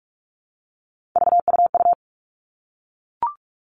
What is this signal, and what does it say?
Morse code, heard as a keyed tone, about 44 wpm, 700 Hz. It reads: VUU
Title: All Cut Numbers - 3-Digit 45wpm